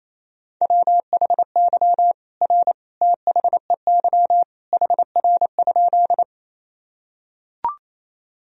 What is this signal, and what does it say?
Morse code, heard as a keyed tone, about 28 wpm, 700 Hz. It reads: WHY R THEY HR?
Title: Sentences from Top 200 Words and Abbreviations 28wpm